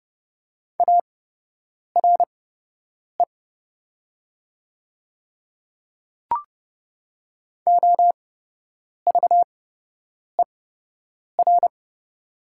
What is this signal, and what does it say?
Morse code, heard as a keyed tone, about 29 wpm, spaced as for 8 wpm, 700 Hz. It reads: ARE OVER
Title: 8x Character Spacing - Top 100 Words - 30wpm